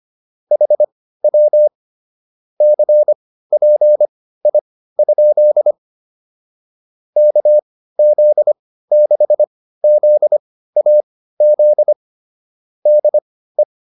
Morse code, heard as a keyed tone, about 25 wpm, spaced as for 15 wpm, 600 Hz. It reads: HW CPI? KZ6ZAZ DE
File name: QSO 302 15